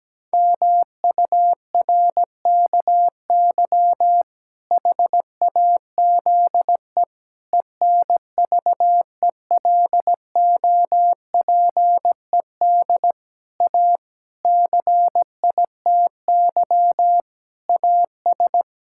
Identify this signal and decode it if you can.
Morse code, heard as a keyed tone, about 17 wpm, 700 Hz. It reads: MURKY HAZE ENVELOPED A CITY AS
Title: Pangram Challenge v2 - 17wpm